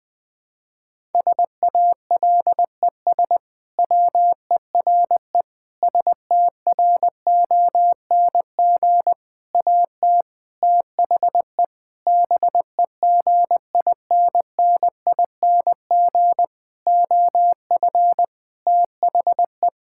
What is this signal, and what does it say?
Morse code, heard as a keyed tone, about 20 wpm, 700 Hz. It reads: SALES WERE STRONG AT THE BEGINNING OF THE
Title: Sentences with 6-Word Phrases - Spoken Beginning - 20wpm